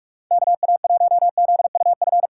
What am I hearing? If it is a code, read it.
Morse code, heard as a keyed tone, about 44 wpm, 700 Hz. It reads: KA1ZUF
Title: USA Call Signs - 6 Characters - 6x Character Spacing - 45wpm